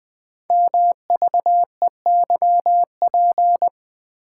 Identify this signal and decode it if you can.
Morse code, heard as a keyed tone, about 20 wpm, 700 Hz. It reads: MVEYP
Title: Code Groups - 5 Letters - 20wpm